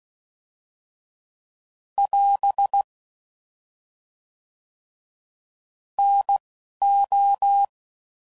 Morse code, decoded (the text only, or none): <AS> N O